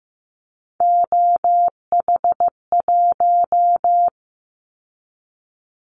Morse code, heard as a keyed tone, about 15 wpm, 700 Hz. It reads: OH1
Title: Callsign Prefixes for Top 50 DXCC - 15wpm